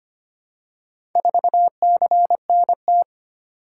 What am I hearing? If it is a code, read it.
Morse code, heard as a keyed tone, about 25 wpm, 700 Hz. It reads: 4CNT
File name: practice / 25wpm / Four Letter-Number Combinations - 25wpm